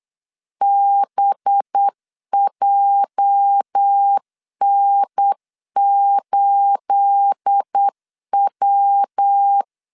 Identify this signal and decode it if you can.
Morse code, heard as a keyed tone, about 8 wpm, 800 Hz. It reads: BJN8W
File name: LEZIONI / 10 SIGNS / 10B INT 8wpm 3534